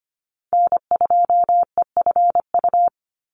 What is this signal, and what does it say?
Morse code, heard as a keyed tone, about 25 wpm, 700 Hz. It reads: N2EFU
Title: USA Call Signs 25wpm